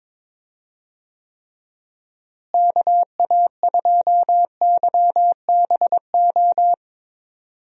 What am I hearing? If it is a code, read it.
Morse code, heard as a keyed tone, about 22 wpm, 700 Hz. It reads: KA2YBO
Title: USA Call Signs - 6 Characters - Repeated 2x - 22wpm